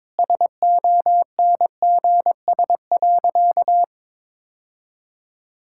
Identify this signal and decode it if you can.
Morse code, heard as a keyed tone, about 22 wpm, 700 Hz. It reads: SONGS.
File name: practice / 22wpm / Easy Sentences with Hints - 22wpm